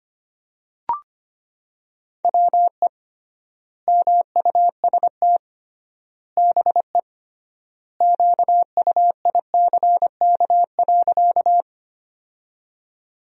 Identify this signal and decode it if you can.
Morse code, heard as a keyed tone, about 25 wpm, 700 Hz. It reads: WE MUST BE QUICK.
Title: 3x Word-Spacing - Sentences from Top 500 Words - 25wpm